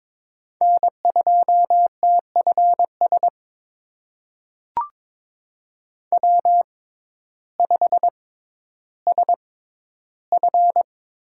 Morse code, heard as a keyed tone, about 22 wpm, spaced as for 12 wpm, 700 Hz. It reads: N2TFS W5SF
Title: USA Call Signs - 5 Characters - 6x Character Spacing - 22wpm